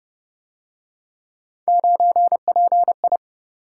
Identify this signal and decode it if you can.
Morse code, heard as a keyed tone, about 30 wpm, 700 Hz. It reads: 9PI